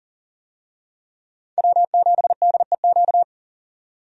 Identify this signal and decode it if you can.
Morse code, heard as a keyed tone, about 38 wpm, 700 Hz. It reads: W7DEQ
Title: USA Call Signs - 5 Characters - 40wpm